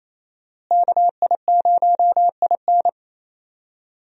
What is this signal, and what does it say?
Morse code, heard as a keyed tone, about 28 wpm, 700 Hz. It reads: KI0IN
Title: USA Call Signs - 5 Characters - Repeated 2x - 28wpm